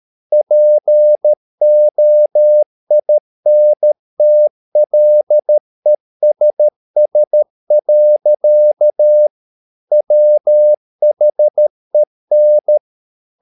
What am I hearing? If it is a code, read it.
Morse code, heard as a keyed tone, about 13 wpm, 600 Hz. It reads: POINTLESS. WHEN
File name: mp3 files / Lexiphiles - 13wpm